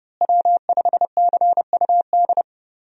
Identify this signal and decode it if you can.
Morse code, heard as a keyed tone, about 32 wpm, 700 Hz. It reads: W5CUD
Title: USA Call Signs - 5 Characters - Repeated 2x - 30wpm